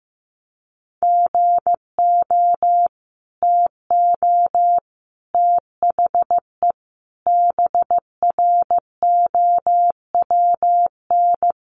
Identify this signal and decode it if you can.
Morse code, heard as a keyed tone, about 15 wpm, 700 Hz. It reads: GO TO THE BROWN
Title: Sentences from Top 1200 Words 15wpm